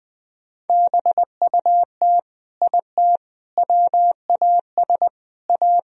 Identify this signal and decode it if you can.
Morse code, heard as a keyed tone, about 21 wpm, 700 Hz. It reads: BUT IT WAS A
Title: Sentences from Top 1000 Words - Review Part 4 - 20wpm